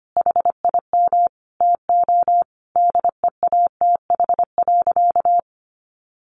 Morse code, heard as a keyed tone, about 25 wpm, 700 Hz. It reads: HIM TO DEATH.
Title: Sentences from Top 1100 Words 25wpm